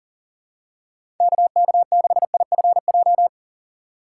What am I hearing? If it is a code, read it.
Morse code, heard as a keyed tone, about 40 wpm, 700 Hz. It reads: KK6IFJ
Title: USA Call Signs - 6 Characters - 6x Character Spacing - 40wpm